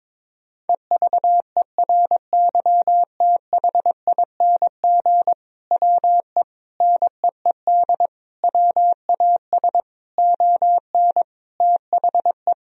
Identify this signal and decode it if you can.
Morse code, heard as a keyed tone, about 22 wpm, 700 Hz. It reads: EVERYTHING WE NEED WAS ON THE
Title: Sentences from Top 1000 Words - Review Part 1 - 22wpm